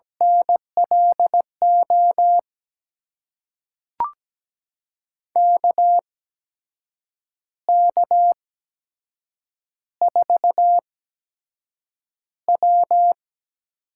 Morse code, heard as a keyed tone, about 17 wpm, spaced as for 7 wpm, 700 Hz. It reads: NLO KK4W
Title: USA Call Signs - 6 Characters - 8x Character Spacing - 17wpm